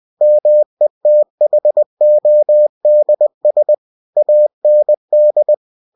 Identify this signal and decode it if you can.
Morse code, heard as a keyed tone, about 20 wpm, 600 Hz. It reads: METHODS AND
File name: cw-books / winnie-the-pooh / winnie-20wpm-page-0122